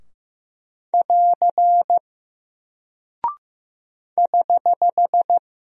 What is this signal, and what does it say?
Morse code, heard as a keyed tone, about 15 wpm, 700 Hz. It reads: <AR> <HH>
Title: Single Character (Letter:Number:Punctuation:Prosign) Alt 15wpm